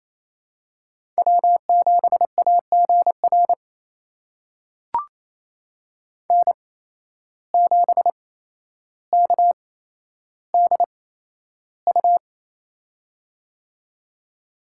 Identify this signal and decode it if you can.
Morse code, heard as a keyed tone, about 28 wpm, spaced as for 12 wpm, 700 Hz. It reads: W7AGR N7KDU
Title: USA Call Signs - 5 Characters - 8x Character Spacing - 28wpm